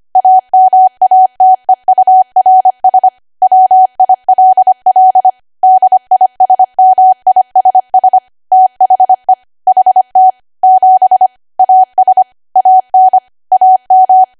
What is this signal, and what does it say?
Morse code, heard as a keyed tone, about 25 wpm, 750 Hz. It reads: AMATEURS WILL DISMISS THE HT 7 AS AN AM